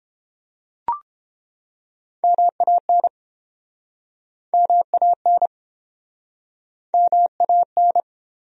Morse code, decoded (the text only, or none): MAN MAN MAN